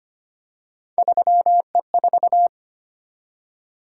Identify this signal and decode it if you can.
Morse code, heard as a keyed tone, about 25 wpm, 700 Hz. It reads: 3E4
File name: Three Letter-Number Combinations - 25wpm